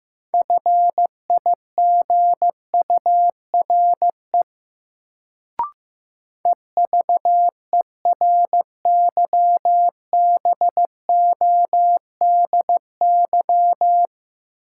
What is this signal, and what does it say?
Morse code, heard as a keyed tone, about 15 wpm, 700 Hz. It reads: FIGURE EVERYBODY
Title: Sentences from Top 1400 Words - New Word Drill Alt - 15wpm